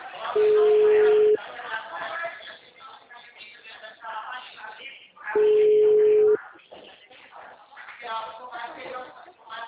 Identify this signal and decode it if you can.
Ringback tone (Europe, 425 Hz), with speech.